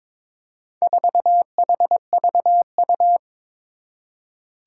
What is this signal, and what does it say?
Morse code, heard as a keyed tone, about 22 wpm, 700 Hz. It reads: 4HVU
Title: Four Challenging Letter-Number Combinations 22wpm